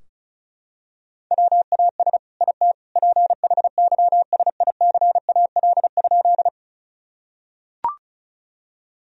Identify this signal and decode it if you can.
Morse code, heard as a keyed tone, about 34 wpm, 700 Hz. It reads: WAS IT PHYSICAL?
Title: Sentences from Top 1000 Words - Review Part 4 - 35wpm